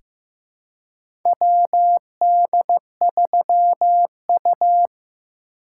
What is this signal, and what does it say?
Morse code, heard as a keyed tone, about 15 wpm, 700 Hz. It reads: WD3U
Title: USA Call Signs - 4 Characters - 2x Character Spacing - 15wpm